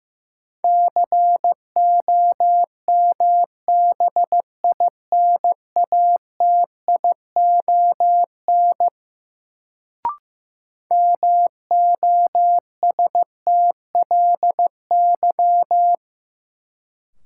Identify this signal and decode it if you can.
Morse code, heard as a keyed tone, about 15 wpm, 700 Hz. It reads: COMBINATION MOSTLY
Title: Sentences from Top 1300 Words - New Word Drill Alt - 15wpm